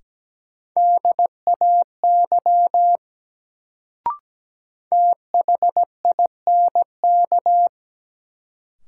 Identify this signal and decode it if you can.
Morse code, heard as a keyed tone, about 17 wpm, 700 Hz. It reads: DAY THINK
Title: Sets of 2 Words - New Word Drill - 17wpm